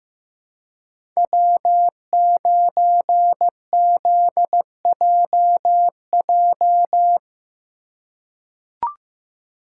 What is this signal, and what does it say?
Morse code, heard as a keyed tone, about 15 wpm, 700 Hz. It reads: W9ZJJ
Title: USA Call Signs - 5 Characters - 4x Character Spacing - 15wpm